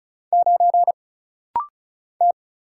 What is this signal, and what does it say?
Morse code, heard as a keyed tone, about 36 wpm, 700 Hz. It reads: 9 T
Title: Single Letter-Number - Rapid-Fire 35wpm